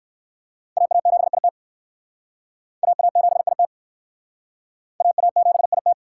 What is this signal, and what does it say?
Morse code, heard as a keyed tone, about 56 wpm, 700 Hz. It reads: AA7IT AA7IT AA7IT
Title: Speed Racing - USA 2x2 Call Signs 45wpm